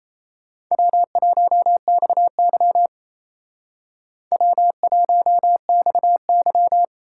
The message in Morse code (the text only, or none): W1XY W1XY